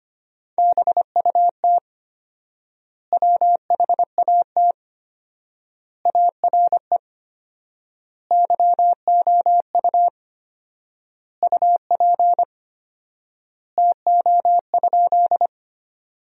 Morse code, heard as a keyed tone, about 25 wpm, 700 Hz. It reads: BUT WHAT ARE YOU UP TO?
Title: 4x Word-Spacing - Sentences from Top 100 Words - 25wpm